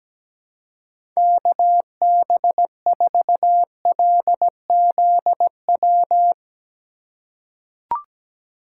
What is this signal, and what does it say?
Morse code, heard as a keyed tone, about 17 wpm, 700 Hz. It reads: KB4LZW
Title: USA Call Signs - 6 Characters - 3x Character Spacing - 17wpm